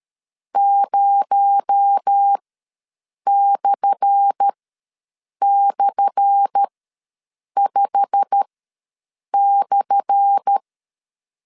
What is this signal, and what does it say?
Morse code, heard as a keyed tone, about 13 wpm, spaced as for 7 wpm, 800 Hz. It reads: 0//5/
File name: LEZIONI / 10 SIGNS / 10A SLASH ws 12wpm 3443